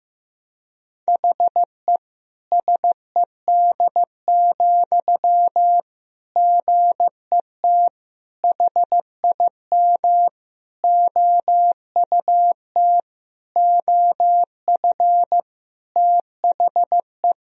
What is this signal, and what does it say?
Morse code, heard as a keyed tone, about 15 wpm, 700 Hz. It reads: HE SED, GET HIM OUT OF THE
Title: Sentences from Top 400 Words and Abbreviations 15wpm